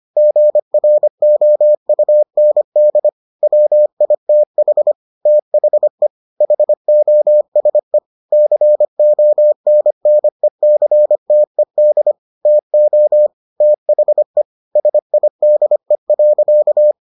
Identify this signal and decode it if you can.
Morse code, heard as a keyed tone, about 25 wpm, 600 Hz. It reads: GROUND WITH THE HOSE CONNECTED TO THE SIDE.